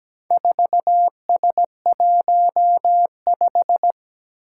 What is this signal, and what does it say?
Morse code, heard as a keyed tone, about 17 wpm, 700 Hz. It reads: 4S15
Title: Four Challenging Letter-Number Combinations 17wpm